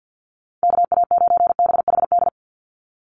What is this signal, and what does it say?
Morse code, heard as a keyed tone, about 50 wpm, 700 Hz. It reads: KA9BHD
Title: Call Signs Repeated 2x - 50wpm